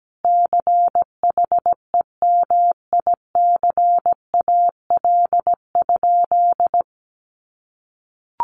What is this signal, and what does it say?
Morse code, heard as a keyed tone, about 17 wpm, 700 Hz. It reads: CHEMICAL?
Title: Sentences from Top 1200 Words 17wpm